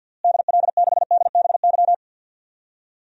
Morse code, heard as a keyed tone, about 48 wpm, 700 Hz. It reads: DL6DBY
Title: DXCC-100 Call Signs - 6 Characters - Repeated 3x - 50wpm